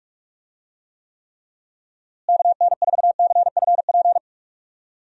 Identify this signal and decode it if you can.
Morse code, heard as a keyed tone, about 46 wpm, 700 Hz. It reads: KN4CFP
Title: USA Call Signs - 6 Characters - Repeated 3x - 45wpm